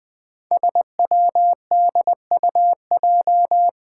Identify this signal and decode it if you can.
Morse code, heard as a keyed tone, about 21 wpm, 700 Hz. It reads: SWDUJ
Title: Five Challenging Letter-Number Combinations 20wpm